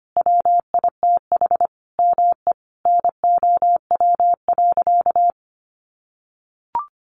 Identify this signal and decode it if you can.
Morse code, heard as a keyed tone, about 25 wpm, 700 Hz. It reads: WITH ME NOW.
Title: Sentences from Top 1500 Words 25wpm